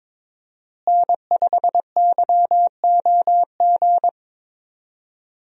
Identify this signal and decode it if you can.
Morse code, heard as a keyed tone, about 22 wpm, 700 Hz. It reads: N5YOG
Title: USA Call Signs - 5 Characters - Repeated 2x - 22wpm